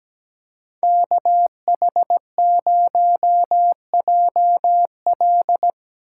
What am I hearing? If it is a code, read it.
Morse code, heard as a keyed tone, about 17 wpm, 700 Hz. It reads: KH0JL
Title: USA Call Signs - 8x Character Spacing - 17wpm